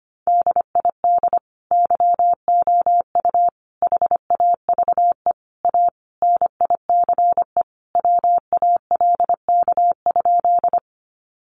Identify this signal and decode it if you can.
Morse code, heard as a keyed tone, about 25 wpm, 700 Hz. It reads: DID YOU HAVE A NICE WALK?
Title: Sentences from Top 1100 Words 25wpm